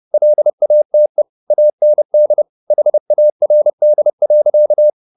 Morse code, heard as a keyed tone, about 30 wpm, 600 Hz. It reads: LATE AND HARD.
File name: the old man and the sea / the old man and the sea 1 30wpm